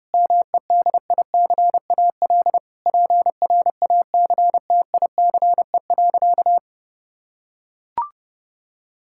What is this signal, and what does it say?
Morse code, heard as a keyed tone, about 30 wpm, 700 Hz. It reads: MEDICAL PRACTICE.
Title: Sentences from Top 1000 Words - Review Part 5 - 30wpm